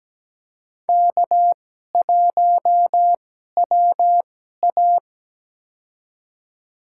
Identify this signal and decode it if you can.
Morse code, heard as a keyed tone, about 17 wpm, spaced as for 12 wpm, 700 Hz. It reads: K1WA
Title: USA Call Signs - 2x Character Spacing - 17wpm